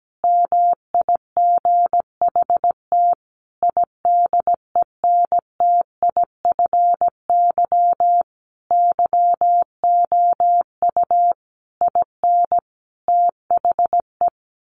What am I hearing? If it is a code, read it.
Morse code, heard as a keyed tone, about 17 wpm, 700 Hz. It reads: MIGHT IDENTIFY YOU IN THE
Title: Sentences from Top 900 Words 17wpm